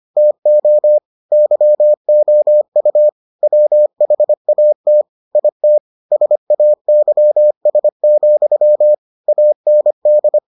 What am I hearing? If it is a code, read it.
Morse code, heard as a keyed tone, about 25 wpm, 600 Hz. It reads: TO YOU WHAT IT SAYS, AND